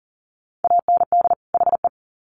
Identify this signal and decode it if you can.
Morse code, heard as a keyed tone, about 40 wpm, 700 Hz. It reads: AND HE
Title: Sets of 2 Words 40wpm